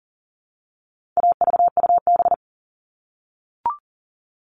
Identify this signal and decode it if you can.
Morse code, heard as a keyed tone, about 40 wpm, 700 Hz. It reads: AVUB
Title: All Cut Numbers - 4-Digit 40wpm